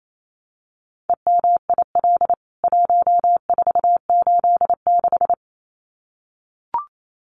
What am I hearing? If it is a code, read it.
Morse code, heard as a keyed tone, about 28 wpm, 700 Hz. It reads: EMIL 1486
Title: CWT Exchange 28wpm